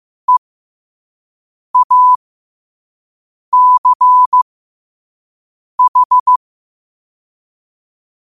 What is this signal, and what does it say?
Morse code, heard as a keyed tone, about 15 wpm, spaced as for 5 wpm, 1000 Hz. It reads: EACH
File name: qotdfiles / 5wpm